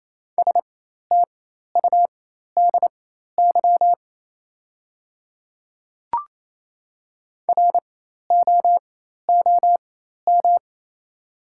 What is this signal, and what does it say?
Morse code, heard as a keyed tone, about 27 wpm, spaced as for 13 wpm, 700 Hz. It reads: STUDY ROOM
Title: 4x Character Spacing - Top 300 Words - 28wpm